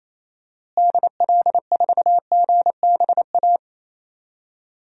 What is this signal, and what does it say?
Morse code, heard as a keyed tone, about 28 wpm, 700 Hz. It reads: DL4GBA
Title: DXCC-100 Call Signs - 6 Characters - Repeated 3x - 28wpm